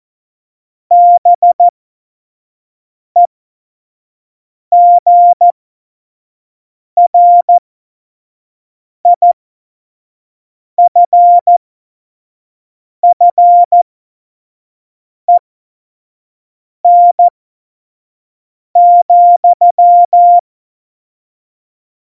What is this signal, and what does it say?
Morse code, heard as a keyed tone, about 14 wpm, spaced as for 5 wpm, 700 Hz. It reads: BEGRIFFEN,
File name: Morsekurs / Witze / Witz 11 Juli 17, Tempo 25 F70